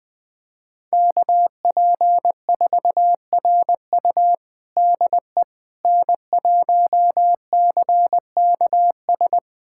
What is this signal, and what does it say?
Morse code, heard as a keyed tone, about 21 wpm, 700 Hz. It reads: KP4RU DE N1CKS
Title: USA Callsign DE Callsign - 5 Characters - 20wpm